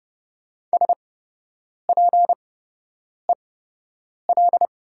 Morse code, heard as a keyed tone, about 31 wpm, spaced as for 8 wpm, 700 Hz. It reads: SPEL
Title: 8x Character Spacing - Top 200 Words - 30wpm